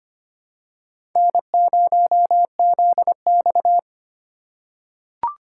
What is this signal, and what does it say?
Morse code, heard as a keyed tone, about 25 wpm, 700 Hz. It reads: N0ZX
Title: USA Call Signs - 4 Characters - 4x Character Spacing - 25wpm